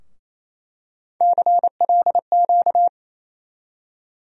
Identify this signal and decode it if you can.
Morse code, heard as a keyed tone, about 28 wpm, 700 Hz. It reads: CLQ
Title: USA Call Sign Postfixes - 3 Letters 28wpm